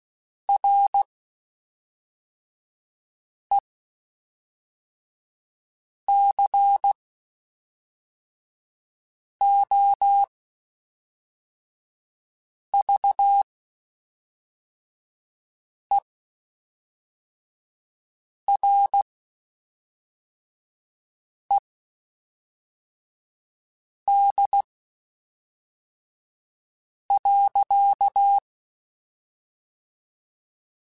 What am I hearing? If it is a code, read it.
Morse code, heard as a keyed tone, about 16 wpm, spaced as for 3 wpm, 800 Hz. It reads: RECOVERED.
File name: Quotes Spaced / Quotes Spaced-mp3s / Character sp @5